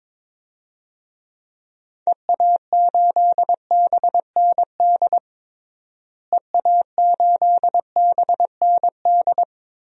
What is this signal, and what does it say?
Morse code, heard as a keyed tone, about 22 wpm, 700 Hz. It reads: EA8BND EA8BND
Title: DXCC-100 Call Signs - 6 Characters - Repeated 3x - 22wpm